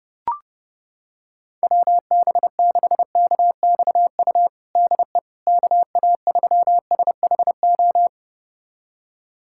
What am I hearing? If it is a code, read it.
Morse code, heard as a keyed tone, about 32 wpm, 700 Hz. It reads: WB6KXU DE KA3SHO
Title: USA Callsign DE Callsign 30wpm